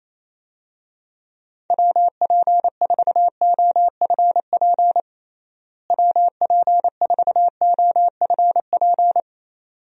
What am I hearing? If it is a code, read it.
Morse code, heard as a keyed tone, about 28 wpm, 700 Hz. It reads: WP4OFP WP4OFP
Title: USA Call Signs - 6 Characters - Repeated 2x - 28wpm